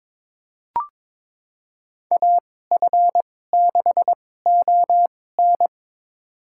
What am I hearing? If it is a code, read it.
Morse code, heard as a keyed tone, about 22 wpm, spaced as for 16 wpm, 700 Hz. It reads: AF6ON
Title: USA Call Signs - 2x Character Spacing - 22wpm